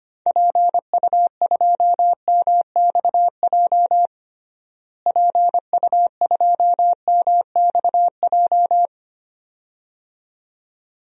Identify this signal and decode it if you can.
Morse code, heard as a keyed tone, about 25 wpm, 700 Hz. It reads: PU2MXJ PU2MXJ
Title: DXCC-100 Call Signs - 6 Characters - Repeated 2x - 25wpm